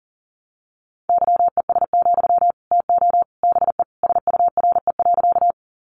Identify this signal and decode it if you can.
Morse code, heard as a keyed tone, about 40 wpm, 700 Hz. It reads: YES, TO BE SURE.
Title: Sentences from Top 500 Words 40wpm